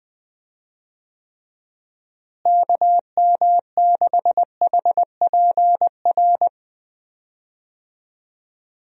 Morse code, heard as a keyed tone, about 20 wpm, 700 Hz. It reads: KM6HPR
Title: USA Call Signs - 6 Characters - Repeated 3x - 20wpm